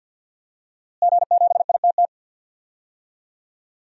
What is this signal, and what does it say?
Morse code, heard as a keyed tone, about 52 wpm, 700 Hz. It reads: G7ITT
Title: DXCC-100 Call Signs - 5 Characters - Repeated 2x - 50wpm